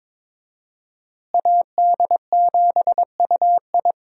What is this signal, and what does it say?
Morse code, heard as a keyed tone, about 22 wpm, 700 Hz. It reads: AD7UI
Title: USA Call Signs - 5 Characters - Repeated 3x - 22wpm